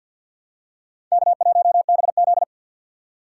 Morse code, heard as a keyed tone, about 52 wpm, 700 Hz. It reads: K1BZ
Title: K1USN SST QSO Practice - 50wpm